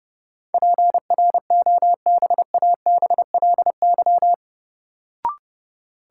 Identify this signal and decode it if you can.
Morse code, heard as a keyed tone, about 30 wpm, 700 Hz. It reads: PROBABLY
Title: Sentences from Top 700 Words - New Word Drill - 30wpm